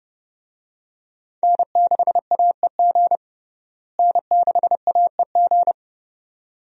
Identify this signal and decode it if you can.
Morse code, heard as a keyed tone, about 30 wpm, 700 Hz. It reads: N6AEG N6AEG
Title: USA Call Signs - 5 Characters - Repeated 2x - 30wpm